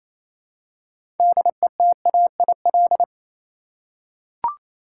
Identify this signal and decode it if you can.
Morse code, heard as a keyed tone, about 29 wpm, 700 Hz. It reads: DETAIL
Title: Six Letter Words - 28wpm